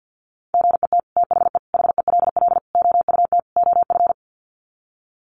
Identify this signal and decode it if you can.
Morse code, heard as a keyed tone, about 50 wpm, 700 Hz. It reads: GET THE HELL OUT OF
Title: Sets of 5 Words Encore2 50wpm